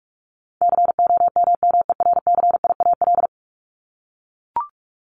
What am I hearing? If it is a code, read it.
Morse code, heard as a keyed tone, about 44 wpm, 700 Hz. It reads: COMMERCIAL
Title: Top 501-750 Words 45wpm